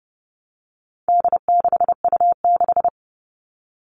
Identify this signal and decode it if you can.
Morse code, heard as a keyed tone, about 30 wpm, 700 Hz. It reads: D6U6
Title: All Cut Numbers - 4-Digit 30wpm